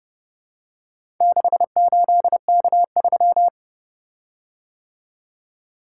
Morse code, heard as a keyed tone, about 31 wpm, 700 Hz. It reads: 68K3